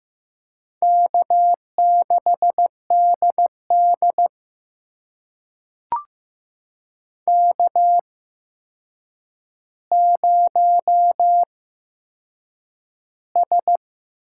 Morse code, heard as a keyed tone, about 15 wpm, spaced as for 7 wpm, 700 Hz. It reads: K6DD K0S
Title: USA Call Signs - 4 Characters - 8x Character Spacing - 15wpm